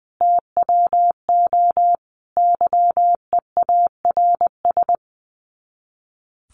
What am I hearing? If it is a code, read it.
Morse code, heard as a keyed tone, about 20 wpm, 700 Hz. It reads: TWO YEARS
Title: Sets of 5 Words Encore2 20wpm